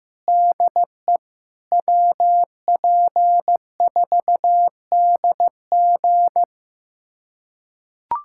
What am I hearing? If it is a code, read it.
Morse code, heard as a keyed tone, about 15 wpm, 700 Hz. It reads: DE WP4DG
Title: USA Callsign DE Callsign - 5 Characters - 15wpm